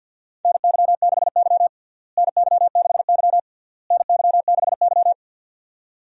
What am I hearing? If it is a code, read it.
Morse code, heard as a keyed tone, about 53 wpm, 700 Hz. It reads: NY6Y NY6Y NY6Y
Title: USA Call Signs - 4 Characters - Repeated 3x - 50wpm